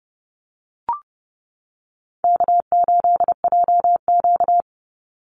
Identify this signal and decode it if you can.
Morse code, heard as a keyed tone, about 30 wpm, 700 Hz. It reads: K8JQ
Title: Easy Call Signs - 30wpm